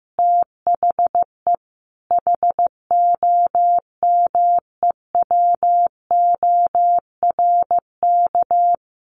Morse code, heard as a keyed tone, about 15 wpm, 700 Hz. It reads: THE HOMEWORK
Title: Easy Phrases 15wpm